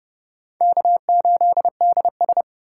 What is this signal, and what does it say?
Morse code, heard as a keyed tone, about 29 wpm, 700 Hz. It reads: K8DS